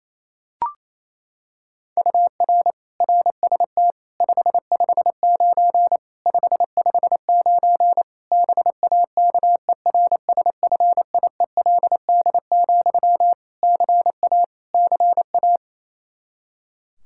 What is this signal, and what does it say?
Morse code, heard as a keyed tone, about 28 wpm, 700 Hz. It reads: UR RST 559 559 BAKERSFIELD, CA CA
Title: K1USN SST QSO Practice 28wpm